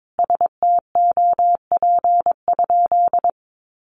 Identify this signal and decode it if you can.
Morse code, heard as a keyed tone, about 22 wpm, 700 Hz. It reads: STOP?
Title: Sentences from Top 1300 Words 22wpm